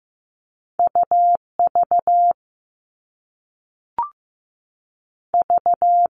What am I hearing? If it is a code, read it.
Morse code, heard as a keyed tone, about 15 wpm, 700 Hz. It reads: UV V
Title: All Cut Numbers - 2-Digit 15wpm